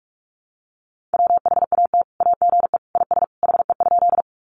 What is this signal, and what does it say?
Morse code, heard as a keyed tone, about 44 wpm, 700 Hz. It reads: WHAT AGE IS HE?
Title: Sentences from Top 500 Words 45wpm